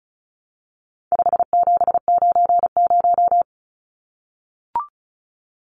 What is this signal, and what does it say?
Morse code, heard as a keyed tone, about 34 wpm, 700 Hz. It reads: 5790
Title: Numbers - 4-Digit 35wpm